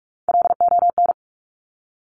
Morse code, heard as a keyed tone, about 46 wpm, 700 Hz. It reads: LON
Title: Top Three Letter Combinations in Words 45wpm